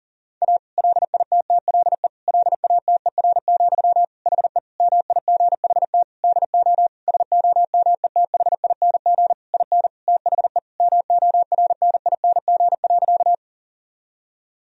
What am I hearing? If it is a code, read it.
Morse code, heard as a keyed tone, about 40 wpm, 700 Hz. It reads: A LITTLE LATER, HE MIGHT DO SOMETHING IN THE MORNING.
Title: Sentences from Top 1000 Words - Review Part 4 - 40wpm